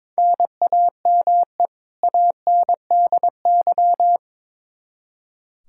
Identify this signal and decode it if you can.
Morse code, heard as a keyed tone, about 21 wpm, 700 Hz. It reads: NAME ANDY
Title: Practice QSO - Exchange 1 - Beginning - 22wpm